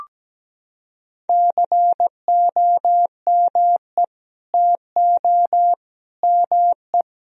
Morse code, heard as a keyed tone, about 17 wpm, 700 Hz. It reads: COME TO ME
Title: Sentences from Top 1000 Words - Review Part 4 - 17wpm